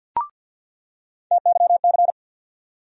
Morse code, heard as a keyed tone, about 49 wpm, 700 Hz. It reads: TYC
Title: USA Call Sign Postfixes - 3 Letters 50wpm